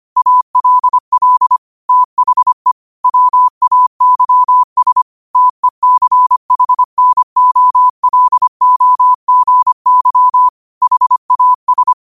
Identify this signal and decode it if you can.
Morse code, heard as a keyed tone, about 25 wpm, 1000 Hz. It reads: ALL THE WAYS TECHNOLOGY HAS